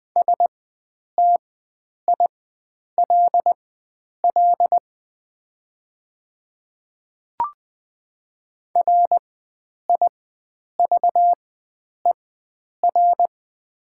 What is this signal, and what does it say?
Morse code, heard as a keyed tone, about 20 wpm, spaced as for 9 wpm, 700 Hz. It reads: STILL RIVER
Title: 4x Character Spacing - Top 500 Words - 20wpm